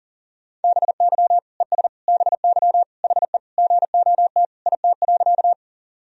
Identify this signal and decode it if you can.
Morse code, heard as a keyed tone, about 40 wpm, 700 Hz. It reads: BY ES BY HE GOT IT.
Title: Sentences from Top 300 Words and Abbreviations 40wpm